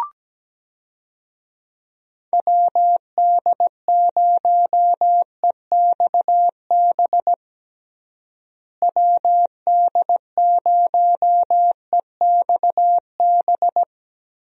Morse code, heard as a keyed tone, about 17 wpm, 700 Hz. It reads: WD0EXB WD0EXB
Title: USA Call Signs - 6 Characters - Repeated 2x - 17wpm